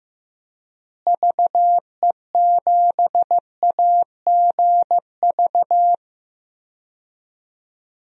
Morse code, heard as a keyed tone, about 15 wpm, 700 Hz. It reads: VE7AGV
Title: DXCC-100 Call Signs - 6 Characters - Repeated 3x - 15wpm